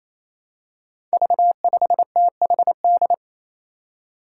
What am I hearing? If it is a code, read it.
Morse code, heard as a keyed tone, about 28 wpm, 700 Hz. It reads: V5THD
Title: Five Challenging Letter-Number Combinations 28wpm